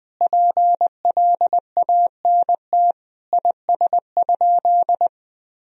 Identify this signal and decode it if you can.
Morse code, heard as a keyed tone, about 20 wpm, 700 Hz. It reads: PLANT IS?
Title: Sentences from Top 1000 Words - Review Part 1 - 20wpm